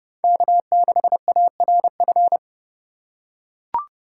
Morse code, heard as a keyed tone, about 30 wpm, 700 Hz. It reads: K6ARF
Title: USA Call Signs - 5 Characters - 3x Character Spacing - 30wpm